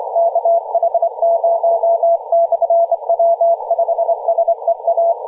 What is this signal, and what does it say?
Morse code, heard as a keyed tone, about 25 wpm, 690 Hz, with radio noise. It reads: KH0/W5SEA
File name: posts / 2016-aug-20-0742z-kh0-w5sea-18-071mhz-cw